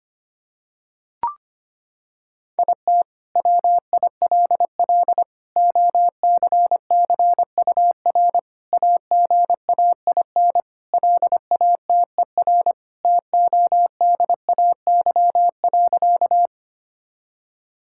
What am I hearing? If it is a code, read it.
Morse code, heard as a keyed tone, about 25 wpm, 700 Hz. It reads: IT WILL OCCUR AGAIN LATER TODAY.
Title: Sentences from Top 1000 Words - Review Part 6 - 25wpm